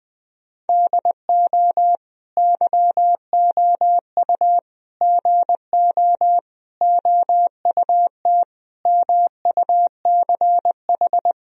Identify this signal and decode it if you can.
Morse code, heard as a keyed tone, about 21 wpm, 700 Hz. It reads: DO YOU GO OUT MUCH
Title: Sentences from Top 1000 Words - Review Part 4 - 20wpm